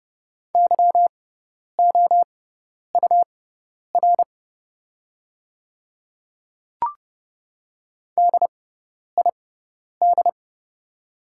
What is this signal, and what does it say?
Morse code, heard as a keyed tone, about 30 wpm, spaced as for 10 wpm, 700 Hz. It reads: YOUR DID
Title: 6x Character Spacing - Top 100 Words - 30wpm